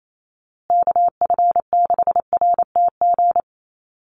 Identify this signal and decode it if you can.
Morse code, heard as a keyed tone, about 28 wpm, 700 Hz. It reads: KF6RTG
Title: USA Call Signs Repeated 3x 28wpm